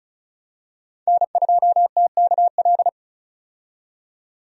Morse code, heard as a keyed tone, about 36 wpm, 700 Hz. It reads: N2TKL
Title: USA Call Signs - 5 Characters - 35wpm